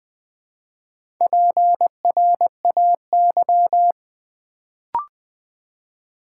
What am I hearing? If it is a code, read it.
Morse code, heard as a keyed tone, about 20 wpm, 700 Hz. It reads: PRAY
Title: Build A Word - 2 Letters At A Time - 4-Letter Words - 20wpm